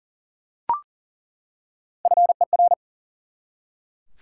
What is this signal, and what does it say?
Morse code, heard as a keyed tone, about 40 wpm, 700 Hz. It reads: FER
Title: Top 90 QSO Elements 40wpm